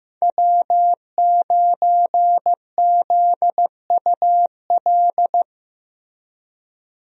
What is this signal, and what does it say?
Morse code, heard as a keyed tone, about 15 wpm, 700 Hz. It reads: W9ZUL
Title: USA Call Signs - 5 Characters - Repeated 3x - 15wpm